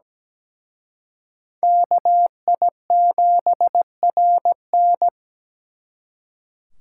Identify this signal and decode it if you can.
Morse code, heard as a keyed tone, about 20 wpm, 700 Hz. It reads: KI7RN
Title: Speed Racing - USA Call Signs 17wpm